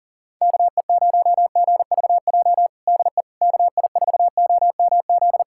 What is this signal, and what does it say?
Morse code, heard as a keyed tone, about 40 wpm, 700 Hz. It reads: KE0GVJ DE KI4OMZ
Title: USA Callsign DE Callsign 40wpm